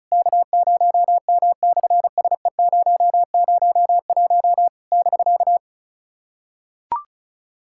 Morse code, heard as a keyed tone, about 33 wpm, 700 Hz. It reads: K0M/SE001 <BK>
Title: SOTA QSO Practice 35wpm